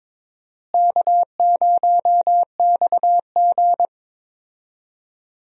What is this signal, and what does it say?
Morse code, heard as a keyed tone, about 22 wpm, 700 Hz. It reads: K0XG
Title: USA Call Signs - 4 Characters - Repeated 3x - 22wpm